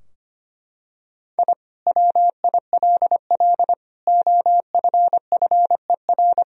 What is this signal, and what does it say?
Morse code, heard as a keyed tone, about 25 wpm, 700 Hz. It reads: I WILL OFFER
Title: Sentences from Top 1000 Words - Review Part 4 - 25wpm